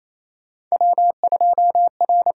Morse code, heard as a keyed tone, about 28 wpm, 700 Hz. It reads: W2R
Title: USA Call Signs - 5 Characters - 6x Character Spacing - 28wpm